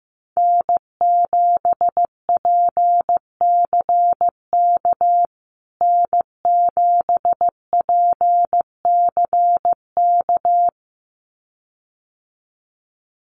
Morse code, heard as a keyed tone, about 15 wpm, 700 Hz. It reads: N7PCK N7PCK
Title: USA Call Signs Repeated 2x 15wpm